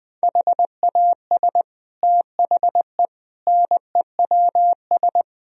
Morse code, heard as a keyed tone, about 20 wpm, 700 Hz. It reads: HAS THE NEWS
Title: Sentences from Top 1000 Words - Review Part 3 - 20wpm